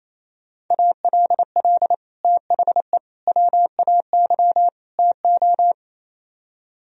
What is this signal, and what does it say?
Morse code, heard as a keyed tone, about 34 wpm, 700 Hz. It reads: ALL THE WAY TO
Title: Speed-Racing - Sets of 4 Words 28wpm